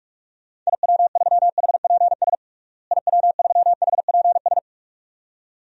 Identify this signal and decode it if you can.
Morse code, heard as a keyed tone, about 42 wpm, 700 Hz. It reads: IW3HPS IW3HPS
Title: DXCC-100 Call Signs - 6 Characters - Repeated 2x - 45wpm